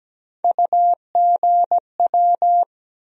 Morse code, heard as a keyed tone, about 17 wpm, 700 Hz. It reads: UGW
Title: USA Call Sign Postfixes - 3 Letters 17wpm